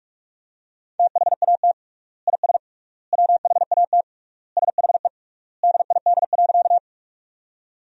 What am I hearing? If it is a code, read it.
Morse code, heard as a keyed tone, about 41 wpm, 700 Hz. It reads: THAT IS WHAT SHE DID.
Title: 3x Word-Spacing - Sentences from Top 100 Words - 45wpm